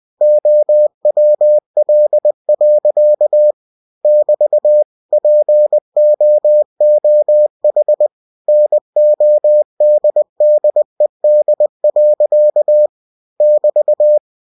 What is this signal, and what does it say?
Morse code, heard as a keyed tone, about 20 wpm, 600 Hz. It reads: OWL. <BT> POOH NODDED. <BT>
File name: cw-books / winnie-the-pooh / winnie-20wpm-page-0031